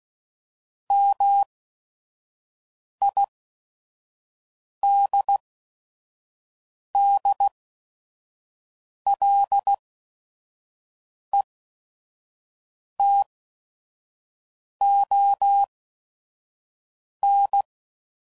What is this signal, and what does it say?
Morse code, heard as a keyed tone, about 16 wpm, spaced as for 5 wpm, 800 Hz. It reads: MIDDLETON
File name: CW Code Groups @5WPM / fromSM's / Cities